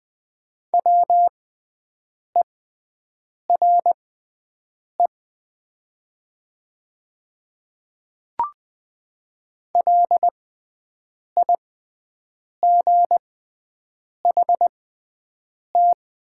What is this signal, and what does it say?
Morse code, heard as a keyed tone, about 20 wpm, spaced as for 7 wpm, 700 Hz. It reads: WERE LIGHT
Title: 6x Character Spacing - Top 200 Words - 20wpm